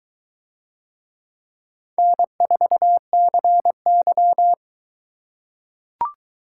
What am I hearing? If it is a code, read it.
Morse code, heard as a keyed tone, about 22 wpm, 700 Hz. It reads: N4CY